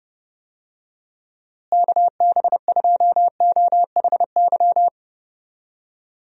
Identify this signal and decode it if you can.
Morse code, heard as a keyed tone, about 32 wpm, 700 Hz. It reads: KB2OHY
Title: USA Call Signs - 6 Characters - Repeated 3x - 30wpm